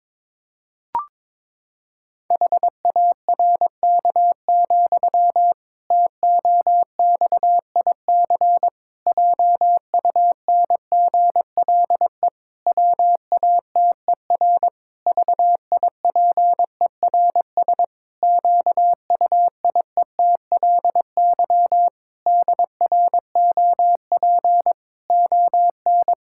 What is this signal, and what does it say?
Morse code, heard as a keyed tone, about 22 wpm, 700 Hz. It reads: HARK, TOXIC JUNGLE WATER VIPERS QUIETLY DROP ON
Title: Pangram Challenge v2 - 22wpm